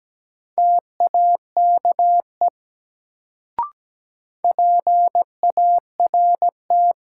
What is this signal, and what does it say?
Morse code, heard as a keyed tone, about 17 wpm, 700 Hz. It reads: TAKE PART
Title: Sentences from Top 200 Words - New Word Drill - 17wpm